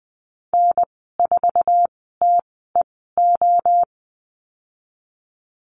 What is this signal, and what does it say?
Morse code, heard as a keyed tone, about 20 wpm, spaced as for 15 wpm, 700 Hz. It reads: N4TEO
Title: 069 Lesson 12g - Letter d - Call Signs 20wpm